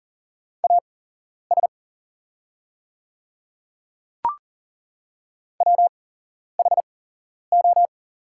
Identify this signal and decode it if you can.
Morse code, heard as a keyed tone, about 40 wpm, spaced as for 11 wpm, 700 Hz. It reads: AS WHO
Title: 8x Character Spacing - Top 100 Words - 40wpm